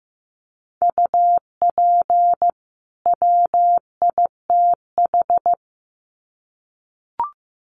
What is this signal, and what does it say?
Morse code, heard as a keyed tone, about 15 wpm, 700 Hz. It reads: UP WITH
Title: Sets of 2 Words 15wpm